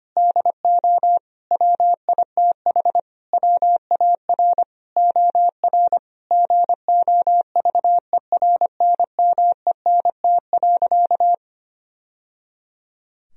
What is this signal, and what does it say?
Morse code, heard as a keyed tone, about 25 wpm, 700 Hz. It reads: DO WITH WAR OR GOVERNMENT.
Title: Sentences from Top 1000 Words - Review Part 3 - 25wpm